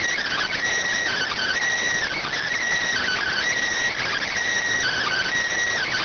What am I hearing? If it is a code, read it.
Emergency Alert System (SAME) tones, heard as harsh buzzing bursts. It is heard from 1.1 to 5.3 s.